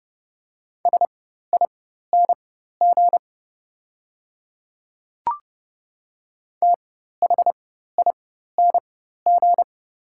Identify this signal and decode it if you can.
Morse code, heard as a keyed tone, about 30 wpm, spaced as for 14 wpm, 700 Hz. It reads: SING THING
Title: 4x Character Spacing - Top 500 Words - 30wpm